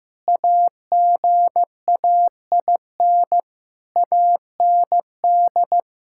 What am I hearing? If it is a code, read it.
Morse code, heard as a keyed tone, about 15 wpm, 700 Hz. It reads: AGAIN AND
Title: Pangram Challenge v2 - 15wpm